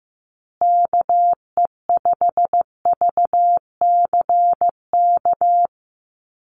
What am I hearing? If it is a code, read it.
Morse code, heard as a keyed tone, about 15 wpm, 700 Hz. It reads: KE5VCK
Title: USA Call Signs Repeated 2x 15wpm